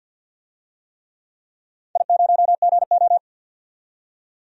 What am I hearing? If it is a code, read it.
Morse code, heard as a keyed tone, about 47 wpm, 700 Hz. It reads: I0GO
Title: DXCC-100 Call Signs - 4 Characters - 50wpm